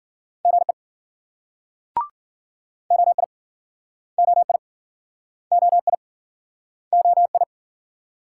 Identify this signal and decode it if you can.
Morse code, heard as a keyed tone, about 60 wpm, 700 Hz. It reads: GE OI OI OI OI